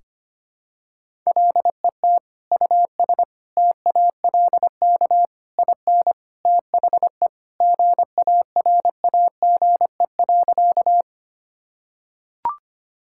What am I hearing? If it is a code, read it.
Morse code, heard as a keyed tone, about 25 wpm, 700 Hz. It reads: LET US TALK IN THE GARAGE.
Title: Sentences from Top 1600 Words 25wpm